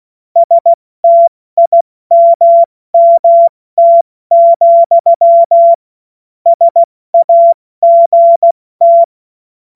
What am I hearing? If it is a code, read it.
Morse code, heard as a keyed tone, about 16 wpm, 680 Hz. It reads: STIMMT, SAGT
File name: Morsekurs / Witze / Witz 08. Februar 2022 70 F80